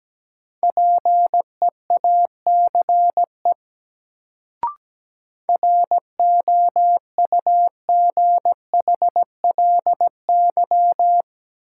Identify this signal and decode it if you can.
Morse code, heard as a keyed tone, about 17 wpm, 700 Hz. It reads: PEACE ROUGHLY
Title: Sentences from Top 1500 Words - New Word Drill - 17wpm